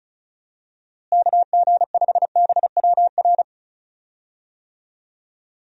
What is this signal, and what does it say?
Morse code, heard as a keyed tone, about 36 wpm, 700 Hz. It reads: KG5BWR
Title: USA Call Signs - 6 Characters - Repeated 3x - 35wpm